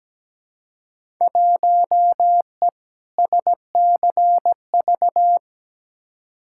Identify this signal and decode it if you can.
Morse code, heard as a keyed tone, about 17 wpm, 700 Hz. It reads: 1E SCV
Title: Field Day Exchanges - 17wpm